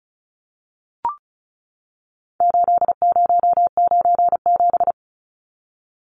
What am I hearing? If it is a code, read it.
Morse code, heard as a keyed tone, about 34 wpm, 700 Hz. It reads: 8097
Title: Numbers - 4-Digit 35wpm